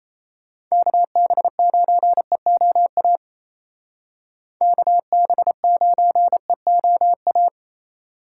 Morse code, heard as a keyed tone, about 36 wpm, 700 Hz. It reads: KB9EOA KB9EOA